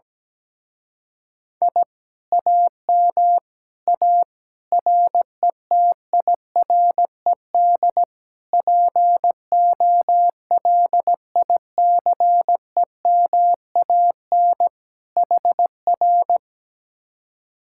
Morse code, heard as a keyed tone, about 17 wpm, 700 Hz. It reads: I AM A RETIRED POLICEMAN HR
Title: Speed-Racing Common QSO Phrases - 17wpm